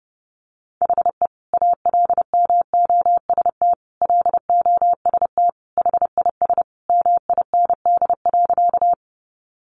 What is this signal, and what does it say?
Morse code, heard as a keyed tone, about 30 wpm, 700 Hz. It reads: HE ALMOST LOST HIS MIND.
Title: Sentences from Top 1400 Words 30wpm